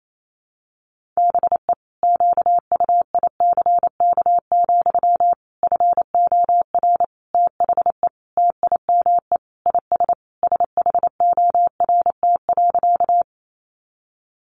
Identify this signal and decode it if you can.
Morse code, heard as a keyed tone, about 28 wpm, 700 Hz. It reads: BE QUICK, FOR THE TIME IS SHORT.
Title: Sentences from Top 500 Words 28wpm